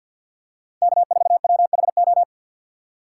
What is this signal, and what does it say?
Morse code, heard as a keyed tone, about 52 wpm, 700 Hz. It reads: K4WHO
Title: USA Call Signs - 5 Characters - 3x Character Spacing - 50wpm